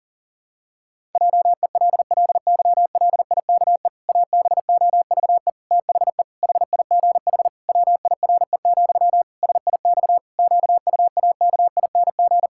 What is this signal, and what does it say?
Morse code, heard as a keyed tone, about 37 wpm, 700 Hz. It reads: JELLYLIKE ABOVE THE HIGH WIRE, SIX QUAKING
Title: Pangram Challenge v2 - 40wpm